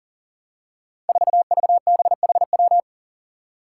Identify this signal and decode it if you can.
Morse code, heard as a keyed tone, about 42 wpm, 700 Hz. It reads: 4VBHW
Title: Five Challenging Letter-Number Combinations 40wpm